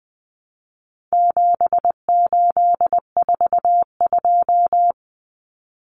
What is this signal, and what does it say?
Morse code, heard as a keyed tone, about 20 wpm, 700 Hz. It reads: 7842
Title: Numbers - 4-Digit 20wpm